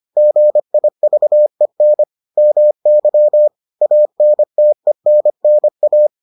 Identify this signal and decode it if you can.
Morse code, heard as a keyed tone, about 25 wpm, 600 Hz. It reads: GIVEN MY ANTENNA